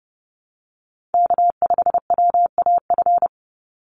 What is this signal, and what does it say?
Morse code, heard as a keyed tone, about 30 wpm, 700 Hz. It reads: K5WAF
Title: Call Signs Repeated 2x - 30wpm